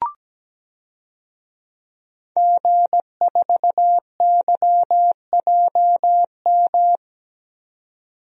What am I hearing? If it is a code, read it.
Morse code, heard as a keyed tone, about 17 wpm, 700 Hz. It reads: G4YJM
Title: DXCC-100 Call Signs - 5 Characters - 17wpm